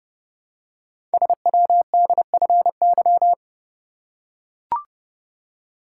Morse code, heard as a keyed tone, about 30 wpm, 700 Hz. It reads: SWDFY
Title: Code Groups - 5 Letters - 30wpm